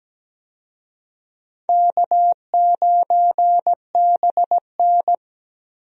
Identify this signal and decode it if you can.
Morse code, heard as a keyed tone, about 17 wpm, 700 Hz. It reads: K9BN
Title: USA Call Signs - 4 Characters - Repeated 2x - 17wpm